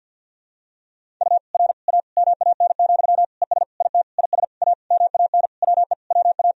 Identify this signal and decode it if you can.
Morse code, heard as a keyed tone, about 52 wpm, 700 Hz. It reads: U R A MAN, ES IT IS A MAN WE WA
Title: Sentences from Top 200 Words and Abbreviations 50wpm